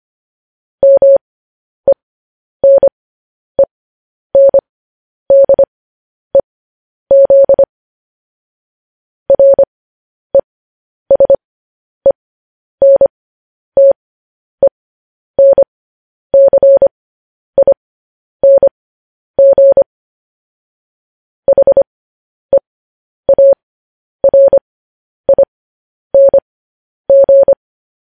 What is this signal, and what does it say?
Morse code, heard as a keyed tone, about 25 wpm, spaced as for 10 wpm, 550 Hz. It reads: MENENDEZ RESENTENCING HEARING
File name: podcast / News headlines-1x-25x10